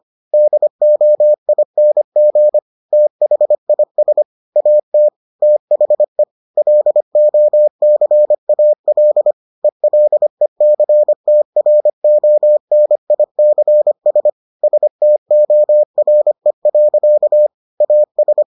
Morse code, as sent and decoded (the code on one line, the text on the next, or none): -.. --- .. -. --. / - .... .. ... / .- - / - .... / .-.. --- -.-. .- .-.. / . .-.. -.-. - .-. --- -. .. -.-. ... / ... - --- .-. .-.-.- / .- ...
DOING THIS AT THE LOCAL ELECTRONICS STORE. AS